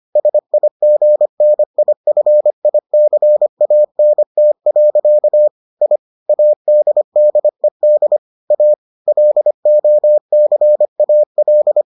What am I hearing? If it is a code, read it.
Morse code, heard as a keyed tone, about 25 wpm, 600 Hz. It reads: SIGNIFICANT. I ADDED A LOCAL